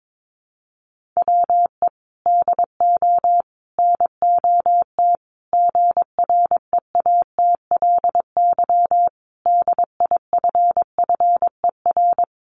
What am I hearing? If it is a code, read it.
Morse code, heard as a keyed tone, about 22 wpm, 700 Hz. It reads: WE DO NOT GREATLY DIFFER
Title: Sentences from Top 1400 Words 22wpm